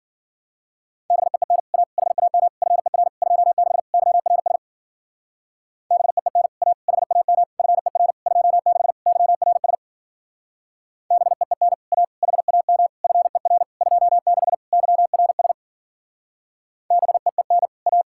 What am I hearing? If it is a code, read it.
Morse code, heard as a keyed tone, about 49 wpm, 700 Hz. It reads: BEEN A HAM FER 26 YRS BEEN A HAM FER 26 YRS BEEN A HAM FER 26 YRS BEEN A
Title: Speed-Racing Common QSO Phrases - 40wpm